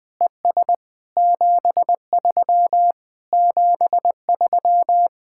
Morse code, heard as a keyed tone, about 20 wpm, 700 Hz. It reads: ES 73 73
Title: Speed-Racing Common QSO Phrases - 17wpm